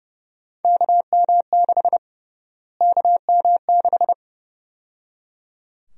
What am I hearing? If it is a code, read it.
Morse code, heard as a keyed tone, about 30 wpm, 700 Hz. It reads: KM6 KM6
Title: USA Call Sign Prefixes - Repeated 2x - 30wpm